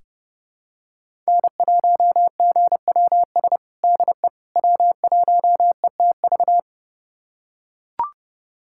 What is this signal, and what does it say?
Morse code, heard as a keyed tone, about 32 wpm, 700 Hz. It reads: N1GWS DE W1ETV
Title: USA Callsign DE Callsign - 5 Characters - 30wpm